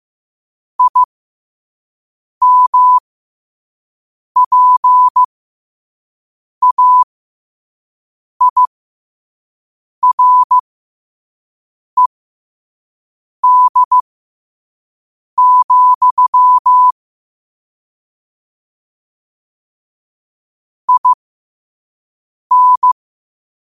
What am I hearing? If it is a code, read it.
Morse code, heard as a keyed tone, about 15 wpm, spaced as for 5 wpm, 1000 Hz. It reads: IMPAIRED, IN
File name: 5wpm